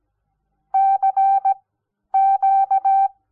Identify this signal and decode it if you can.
Morse code, heard as a keyed tone, about 17 wpm, spaced as for 10 wpm, 780 Hz. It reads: CQ
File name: Onlinelehrgang / morsen / DJ4UF Morsen06